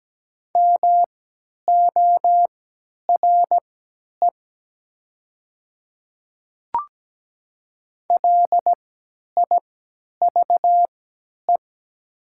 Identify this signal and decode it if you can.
Morse code, heard as a keyed tone, about 17 wpm, spaced as for 10 wpm, 690 Hz. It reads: MORE LIVE
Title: 3x Character Spacing - Top 200 Words - 17wpm